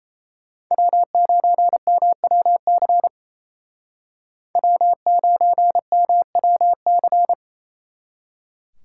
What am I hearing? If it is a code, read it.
Morse code, heard as a keyed tone, about 36 wpm, 700 Hz. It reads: W9MWC W9MWC